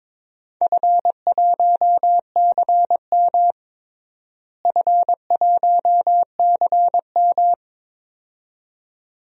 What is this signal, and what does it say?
Morse code, heard as a keyed tone, about 22 wpm, 700 Hz. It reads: F1CM F1CM
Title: DXCC-100 Call Signs - 4 Characters - Repeated 2x - 22wpm